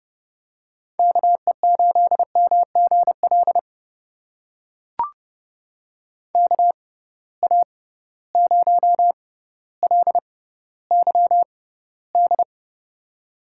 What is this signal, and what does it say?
Morse code, heard as a keyed tone, about 30 wpm, spaced as for 15 wpm, 700 Hz. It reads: KE8MGL KA0LYD
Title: USA Call Signs - 6 Characters - 6x Character Spacing - 30wpm